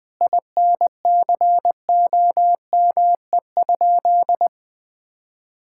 Morse code, heard as a keyed tone, about 20 wpm, 700 Hz. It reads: INCOME?
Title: Sentences from Top 1000 Words - Review Part 5 - 20wpm